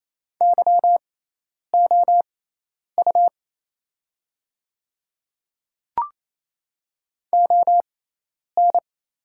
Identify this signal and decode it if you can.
Morse code, heard as a keyed tone, about 28 wpm, spaced as for 10 wpm, 700 Hz. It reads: YOU ON
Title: 6x Character Spacing - Top 100 Words - 28wpm